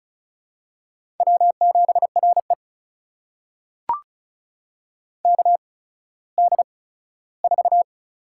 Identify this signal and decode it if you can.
Morse code, heard as a keyed tone, about 36 wpm, spaced as for 17 wpm, 700 Hz. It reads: W7RE KD4
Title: USA Call Signs - 8x Character Spacing - 35wpm